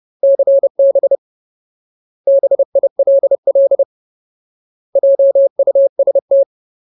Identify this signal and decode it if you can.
Morse code, heard as a keyed tone, about 30 wpm, 550 Hz. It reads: CB BILL JUST
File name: Ultimate Word List / 30WPM Ultimate